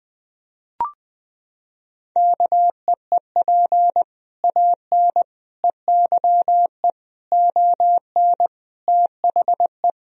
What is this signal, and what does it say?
Morse code, heard as a keyed tone, about 20 wpm, 700 Hz. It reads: KEEP AN EYE ON THE
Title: Sentences from Top 1000 Words - Review Part 3 - 20wpm